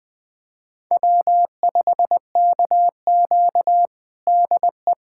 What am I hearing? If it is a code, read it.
Morse code, heard as a keyed tone, about 20 wpm, 700 Hz. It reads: W5KQ DE
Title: USA Callsign DE Callsign - 4 Characters - 20wpm